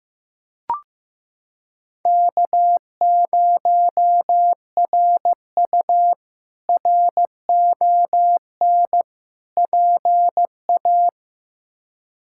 Morse code, heard as a keyed tone, about 15 wpm, 700 Hz. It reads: K0RU RON PA
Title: K1USN SST QSO Practice - 15wpm